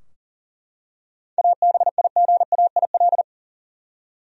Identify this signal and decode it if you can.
Morse code, heard as a keyed tone, about 40 wpm, 700 Hz. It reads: ABIGAIL
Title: USA Common Names - 40wpm